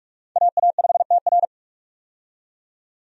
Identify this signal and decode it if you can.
Morse code, heard as a keyed tone, about 46 wpm, 700 Hz. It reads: AA5TR
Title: USA Call Signs - 5 Characters - 45wpm